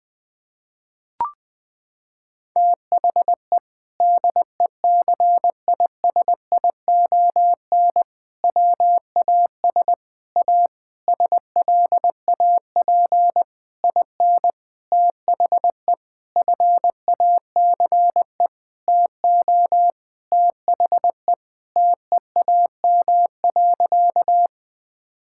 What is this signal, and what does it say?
Morse code, heard as a keyed tone, about 20 wpm, 700 Hz. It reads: THE DECISION WAS A SLAP IN THE FACE TO THE TEAM.
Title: Sentences with 6-Word Phrases - 20wpm